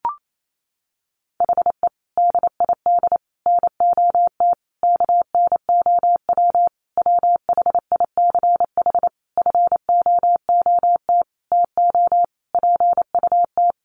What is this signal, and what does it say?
Morse code, heard as a keyed tone, about 28 wpm, 700 Hz. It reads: HE DID NOT KNOW WHICH FOOT TO PUT
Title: Sentences from Top 1300 Words 28wpm